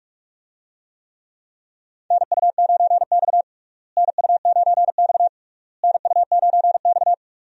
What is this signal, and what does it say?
Morse code, heard as a keyed tone, about 42 wpm, 700 Hz. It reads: NU9X NU9X NU9X
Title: USA Call Signs - 4 Characters - Repeated 3x - 45wpm